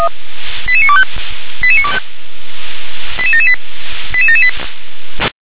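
Five-tone selective call (ZVEI), heard as quick runs of five tones, tones of about 70 ms. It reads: 88925 88925 88778 88778
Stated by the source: Received on SDR, ZVEI code 88952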